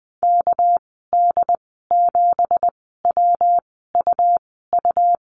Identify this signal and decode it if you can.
Morse code, heard as a keyed tone, about 20 wpm, spaced as for 14 wpm, 700 Hz. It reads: KD7WUU
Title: 200 Lesson 32g - Letter k - Call Signs 20wpm